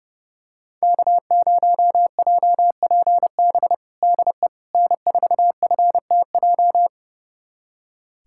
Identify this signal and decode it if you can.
Morse code, heard as a keyed tone, about 32 wpm, 700 Hz. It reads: K0JPB DE N4FTJ
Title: USA Callsign DE Callsign - 5 Characters - 30wpm